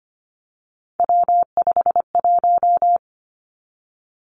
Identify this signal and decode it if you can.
Morse code, heard as a keyed tone, about 25 wpm, 700 Hz. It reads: W51
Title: Three Challenging Letter-Number Combinations 25wpm